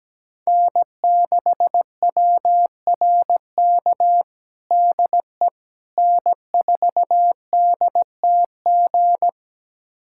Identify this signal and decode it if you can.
Morse code, heard as a keyed tone, about 17 wpm, 700 Hz. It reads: N6WRK DE N4DTG
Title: USA Callsign DE Callsign 17wpm